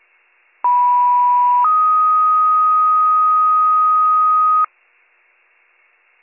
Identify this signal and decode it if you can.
Two-tone page, heard as a one-second tone, then a three-second tone, about 980 then 1250 Hz. It is heard from 0.6 to 4.6 s.